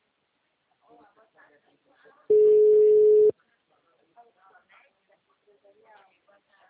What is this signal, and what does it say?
Ringback tone (Europe, 425 Hz), with speech.